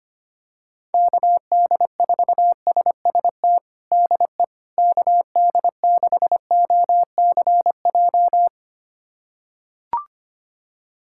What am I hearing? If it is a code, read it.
Morse code, heard as a keyed tone, about 25 wpm, 700 Hz. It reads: KD4SST DE KD6OCJ
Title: USA Callsign DE Callsign - 6 Characters - 25wpm